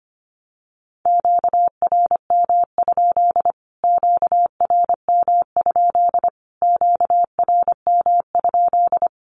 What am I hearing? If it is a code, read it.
Morse code, heard as a keyed tone, about 25 wpm, 700 Hz. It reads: QRM? QRM? QRM?
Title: Q-Codes Repeated 3x 25wpm